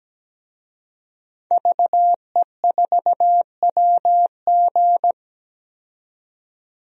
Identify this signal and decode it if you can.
Morse code, heard as a keyed tone, about 17 wpm, 700 Hz. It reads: VE4WG
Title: DXCC-100 Call Signs - 5 Characters - 17wpm